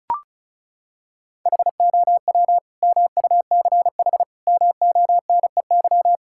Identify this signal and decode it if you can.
Morse code, heard as a keyed tone, about 36 wpm, 700 Hz. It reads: HOW MUCH MONEY
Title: Sentences from Top 1000 Words - Review Part 5 - 35wpm